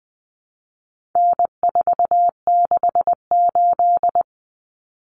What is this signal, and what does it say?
Morse code, heard as a keyed tone, about 20 wpm, 700 Hz. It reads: N468
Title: Common Cut Numbers - 4-Digit 20wpm